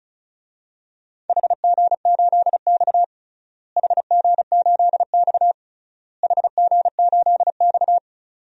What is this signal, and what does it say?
Morse code, heard as a keyed tone, about 34 wpm, 700 Hz. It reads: HG8X HG8X HG8X
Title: DXCC-100 Call Signs - 4 Characters - Repeated 3x - 35wpm